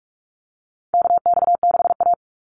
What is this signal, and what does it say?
Morse code, heard as a keyed tone, about 46 wpm, 700 Hz. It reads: KX6A
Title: NA SOTA Chaser Callsigns 45wpm